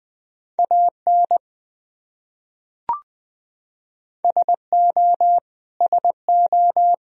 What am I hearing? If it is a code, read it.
Morse code, heard as a keyed tone, about 20 wpm, 700 Hz. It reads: AN SO SO
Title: Top 100 Words Repeated 20wpm